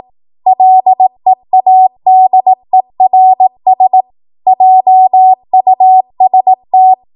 Morse code, heard as a keyed tone, about 18 wpm, 750 Hz. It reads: LEADERS JUST
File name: Morse / 18WPM